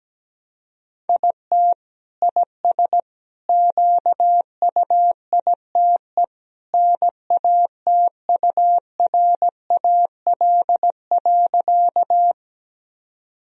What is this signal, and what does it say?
Morse code, heard as a keyed tone, about 17 wpm, 700 Hz. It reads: IT IS QUITE NATURAL.
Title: Sentences from Top 1000 Words - Review Part 4 - 17wpm